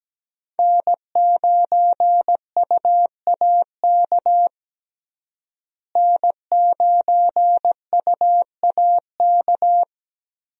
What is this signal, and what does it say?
Morse code, heard as a keyed tone, about 17 wpm, 700 Hz. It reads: N9UAK N9UAK
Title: USA Call Signs - 5 Characters - Repeated 2x - 17wpm